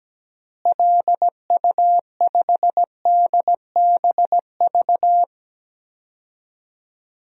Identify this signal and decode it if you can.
Morse code, heard as a keyed tone, about 17 wpm, 700 Hz. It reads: LU5DBV
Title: DXCC-100 Call Signs - 6 Characters - Repeated 3x - 17wpm